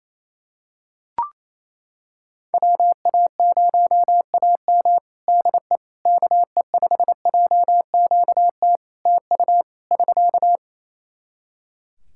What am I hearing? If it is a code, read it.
Morse code, heard as a keyed tone, about 27 wpm, 700 Hz. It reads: WA0AM DE KE5JQT TU <SK>
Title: Practice QSO - Exchange 4 - Ending - 28wpm